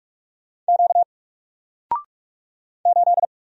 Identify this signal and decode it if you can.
Morse code, heard as a keyed tone, about 44 wpm, 700 Hz. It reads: Q 8
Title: Single Letter-Number Alt 45wpm